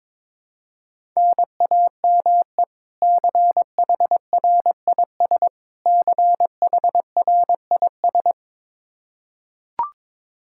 Morse code, heard as a keyed tone, about 22 wpm, 700 Hz. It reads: NAME CHRIS CHRIS
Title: Practice QSO - Exchange 1 - Beginning - 22wpm